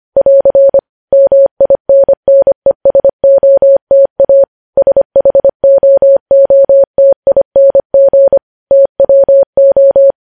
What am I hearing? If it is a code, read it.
Morse code, heard as a keyed tone, about 25 wpm, 550 Hz. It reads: <AR> MINNESOTA SHOOTING TWO
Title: News Headlines at 25